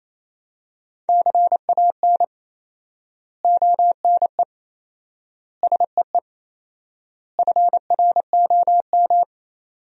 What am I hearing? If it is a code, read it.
Morse code, heard as a keyed tone, about 27 wpm, 700 Hz. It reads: CAN ONE SEE FROM
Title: 4x Word-Spacing - Sentences from Top 100 Words - 28wpm